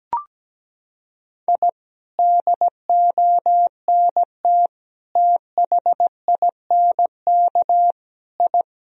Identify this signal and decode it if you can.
Morse code, heard as a keyed tone, about 17 wpm, 700 Hz. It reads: I DONT THINK I
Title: Sentences from Top 1000 Words - Review Part 4 - 17wpm